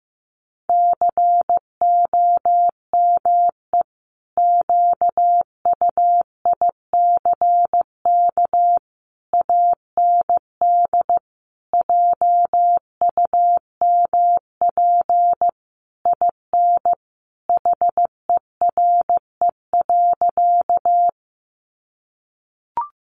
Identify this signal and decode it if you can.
Morse code, heard as a keyed tone, about 15 wpm, 700 Hz. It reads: COME QUICK AND JUMP IN HERE.
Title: Sentences from Top 1300 Words 15wpm